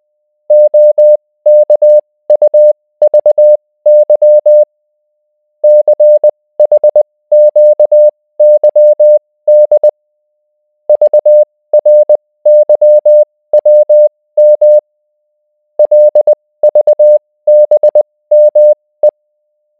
Morse code, heard as a keyed tone, about 20 wpm, spaced as for 16 wpm, 600 Hz. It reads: OKUVY CHQYD VRYWM LVBME